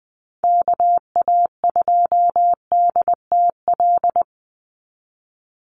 Morse code, heard as a keyed tone, about 20 wpm, 700 Hz. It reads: KA2DTL
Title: USA Call Signs 20wpm